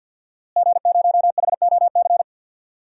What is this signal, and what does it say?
Morse code, heard as a keyed tone, about 50 wpm, 700 Hz. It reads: G0HOC
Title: DXCC-100 Call Signs - 5 Characters - Repeated 3x - 50wpm